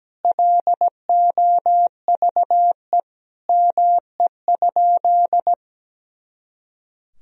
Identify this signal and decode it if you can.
Morse code, heard as a keyed tone, about 17 wpm, 700 Hz. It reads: LOVE ME?
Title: Sentences from Top 400 Words and Abbreviations 17wpm